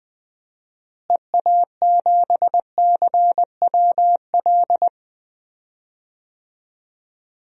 Morse code, heard as a keyed tone, about 20 wpm, 700 Hz. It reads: EA7CWL
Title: DXCC-100 Call Signs - 6 Characters - Repeated 3x - 20wpm